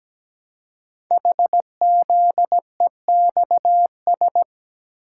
Morse code, heard as a keyed tone, about 17 wpm, 700 Hz. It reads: HZEXS
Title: Code Groups - 5 Letters - 17wpm